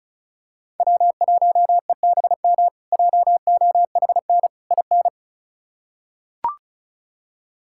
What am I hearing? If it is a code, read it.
Morse code, heard as a keyed tone, about 36 wpm, 700 Hz. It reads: W1EBM JOHN IN
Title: K1USN SST QSO Practice - 35wpm